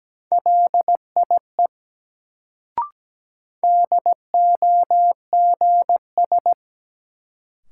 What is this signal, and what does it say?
Morse code, heard as a keyed tone, about 17 wpm, 700 Hz. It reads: LIE DOGS
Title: Sentences from Top 1600 Words - New Word Drill - 17wpm